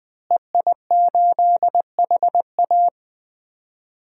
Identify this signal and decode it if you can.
Morse code, heard as a keyed tone, about 21 wpm, 700 Hz. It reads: EI8HA
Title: DXCC-100 Call Signs - 5 Characters - Repeated 3x - 20wpm